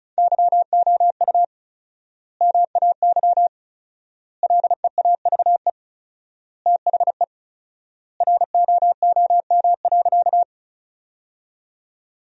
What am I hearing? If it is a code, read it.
Morse code, heard as a keyed tone, about 36 wpm, 700 Hz. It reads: YOU MAY LEAVE THE ROOM.
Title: 4x Word-Spacing - Sentences from Top 400 Words - 35wpm